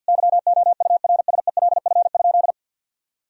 Morse code, heard as a keyed tone, about 50 wpm, 700 Hz. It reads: YOURSELF?
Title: Sentences from Top 1000 Words - Review Part 1 - 50wpm